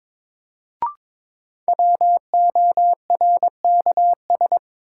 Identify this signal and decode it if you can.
Morse code, heard as a keyed tone, about 23 wpm, 700 Hz. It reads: WORKS
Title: Sentences from Top 1000 Words - Review Part 5 - New Word Drill - 22wpm